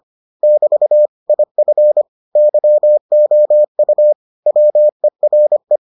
Morse code, heard as a keyed tone, about 25 wpm, 600 Hz. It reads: <BT> IF YOU WERE